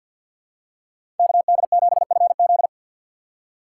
Morse code, heard as a keyed tone, about 52 wpm, 700 Hz. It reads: KD7FZ
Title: USA Call Signs - 3x Character Spacing - 50wpm